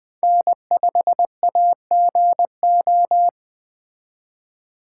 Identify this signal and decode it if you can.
Morse code, heard as a keyed tone, about 24 wpm, 700 Hz. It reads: N5AGO
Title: Speed Racing - USA 1x3 Call Signs 20wpm